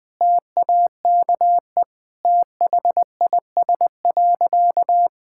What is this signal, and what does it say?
Morse code, heard as a keyed tone, about 20 wpm, 700 Hz. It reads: TAKE THIS.
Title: Sentences with 2-Word Phrases - 20wpm